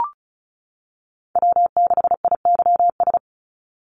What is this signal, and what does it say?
Morse code, heard as a keyed tone, about 34 wpm, 700 Hz. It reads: W6IYS
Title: NA SOTA Chaser Callsigns 35wpm